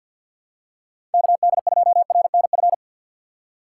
Morse code, heard as a keyed tone, about 53 wpm, 700 Hz. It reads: KD2RNF
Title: USA Call Signs - 6 Characters - 6x Character Spacing - 50wpm